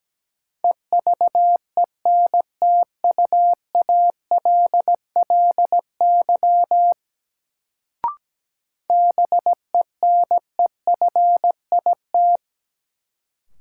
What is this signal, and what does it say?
Morse code, heard as a keyed tone, about 17 wpm, 700 Hz. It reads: EVENTUALLY BENEFIT
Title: Sentences from Top 1000 Words - Review Part 6 - New Word Drill - 17wpm